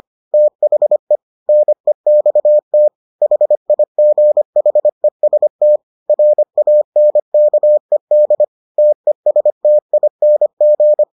Morse code, heard as a keyed tone, about 25 wpm, 600 Hz. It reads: THE NEXT HIGHEST RANKED TESTING